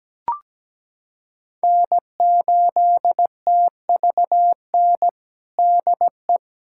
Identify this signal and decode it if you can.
Morse code, heard as a keyed tone, about 17 wpm, 700 Hz. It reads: N8TVN DE
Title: USA Callsign DE Callsign 17wpm